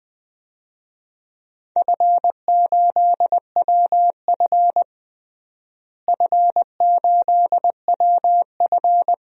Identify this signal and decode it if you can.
Morse code, heard as a keyed tone, about 20 wpm, 700 Hz. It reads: F8WF F8WF
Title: DXCC-100 Call Signs - 4 Characters - Repeated 2x - 20wpm